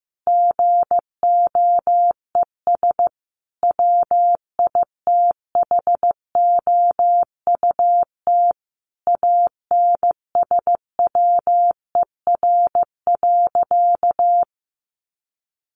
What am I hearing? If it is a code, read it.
Morse code, heard as a keyed tone, about 15 wpm, 700 Hz. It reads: GOES WITHOUT ANSWER.
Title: Sentences from Top 1500 Words 15wpm